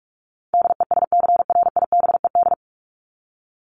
Morse code, heard as a keyed tone, about 44 wpm, 700 Hz. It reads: DESCRIBED
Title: ED Suffix - Top 100 Words with Base Words 45wpm